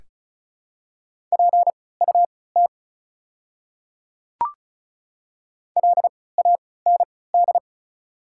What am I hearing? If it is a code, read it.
Morse code, heard as a keyed tone, about 36 wpm, spaced as for 20 wpm, 700 Hz. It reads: PUT LAND
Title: 3x Character Spacing - Top 300 Words - 35wpm